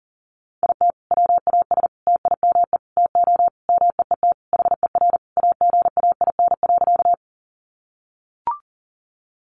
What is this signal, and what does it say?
Morse code, heard as a keyed tone, about 37 wpm, 700 Hz. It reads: IT WAS TIME TO MEET HER AGAIN.
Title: Sentences from Top 1100 Words 40wpm